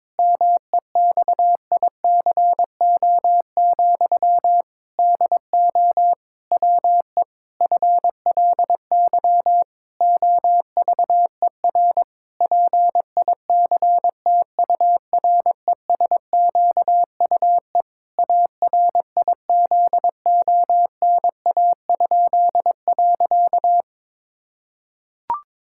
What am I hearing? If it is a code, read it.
Morse code, heard as a keyed tone, about 22 wpm, 700 Hz. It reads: MEXICO, DO WE FLY OVER PICTURESQUE ARIZONA?.
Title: Pangram Challenge v2 - 22wpm